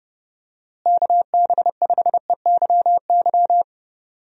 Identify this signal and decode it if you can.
Morse code, heard as a keyed tone, about 30 wpm, 700 Hz. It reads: KB5EYY
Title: USA Call Signs - 6 Characters - Repeated 2x - 30wpm